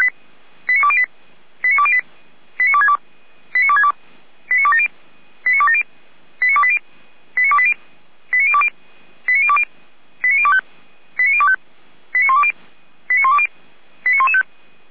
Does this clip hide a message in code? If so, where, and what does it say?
Yes: five-tone selective call (ZVEI) from 0.7 to 14.4 s, heard as quick runs of five tones, tones of about 70 ms. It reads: 89208 89208 89262 89262 89270 89270 89280 89280 89020 89020 89025 89025 89120 89120 89115